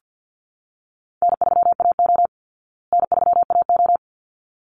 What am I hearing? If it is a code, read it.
Morse code, heard as a keyed tone, about 50 wpm, 700 Hz. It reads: N3AO N3AO
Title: Easy Call Signs Repeated 2x - 50wpm